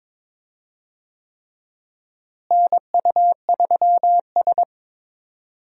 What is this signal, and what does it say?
Morse code, heard as a keyed tone, about 22 wpm, 700 Hz. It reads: NU3S
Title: USA Call Signs - 4 Characters - Repeated 3x - 22wpm